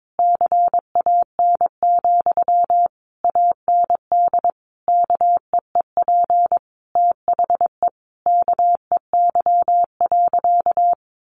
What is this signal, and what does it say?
Morse code, heard as a keyed tone, about 22 wpm, 700 Hz. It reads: CAN, AND KEEP THE KEY.
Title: Sentences from Top 1300 Words 22wpm